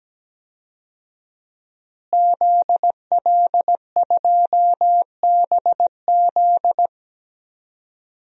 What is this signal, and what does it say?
Morse code, heard as a keyed tone, about 17 wpm, 700 Hz. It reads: ZL2BZ
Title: DXCC-100 Call Signs - 5 Characters - Repeated 3x - 17wpm